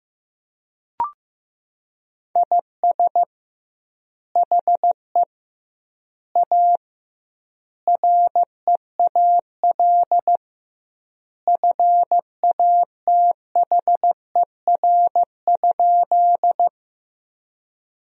Morse code, heard as a keyed tone, about 15 wpm, 700 Hz. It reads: IS HE A REAL FATHER?
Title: 2x Word-Spacing - Sentences from Top 300 Words - 15wpm